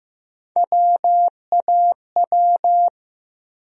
Morse code, heard as a keyed tone, about 15 wpm, 700 Hz. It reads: WAW
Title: USA Call Sign Postfixes - 3 Letters 15wpm